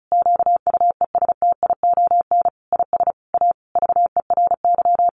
Morse code, heard as a keyed tone, about 34 wpm, 700 Hz. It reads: QUESTION IS A VERY
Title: Sentences from Top 1200 Words 35wpm